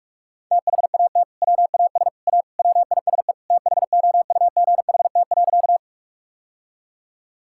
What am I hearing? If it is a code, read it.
Morse code, heard as a keyed tone, about 41 wpm, 700 Hz. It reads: THAT WAS A WISE THOUGHT.
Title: Sentences from Top 1600 Words 45wpm